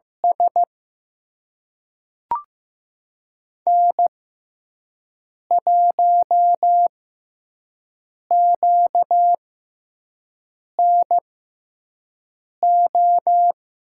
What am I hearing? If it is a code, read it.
Morse code, heard as a keyed tone, about 15 wpm, spaced as for 5 wpm, 700 Hz. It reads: S N1QNO
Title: USA Call Signs - 5 Characters - 6x Character Spacing - 15wpm